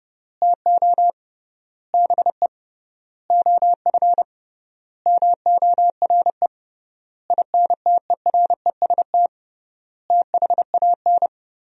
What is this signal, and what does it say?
Morse code, heard as a keyed tone, about 29 wpm, 700 Hz. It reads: TO BE OF MORE INTEREST THAN
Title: 3x Word-Spacing - Sentences from Top 400 Words - 30wpm